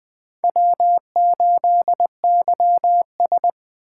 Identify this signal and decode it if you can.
Morse code, heard as a keyed tone, about 20 wpm, 700 Hz. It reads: W8YS
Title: USA Call Signs - 4 Characters - 2x Character Spacing - 20wpm